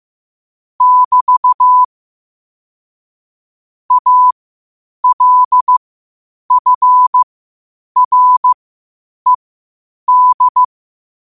Morse code, heard as a keyed tone, about 15 wpm, spaced as for 8 wpm, 1000 Hz. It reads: <BT> ALFRED
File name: qotdfiles / 8wpm